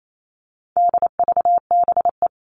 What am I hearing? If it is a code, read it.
Morse code, heard as a keyed tone, about 28 wpm, 700 Hz. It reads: DVBE
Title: All Cut Numbers - 4-Digit 28wpm